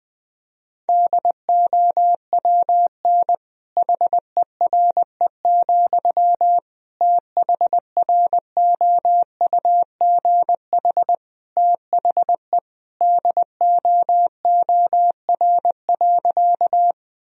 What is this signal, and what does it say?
Morse code, heard as a keyed tone, about 21 wpm, 700 Hz. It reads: DOWN HERE, THROUGH THE DOOR.
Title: Sentences from Top 1000 Words - Review Part 1 - 20wpm